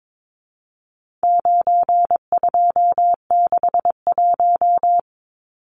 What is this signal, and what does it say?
Morse code, heard as a keyed tone, about 22 wpm, 700 Hz. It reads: 9261
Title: Numbers - 4-Digit 22wpm